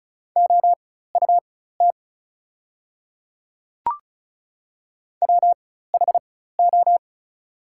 Morse code, heard as a keyed tone, about 36 wpm, spaced as for 16 wpm, 700 Hz. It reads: OUT WHO
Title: 4x Character Spacing - Top 200 Words - 35wpm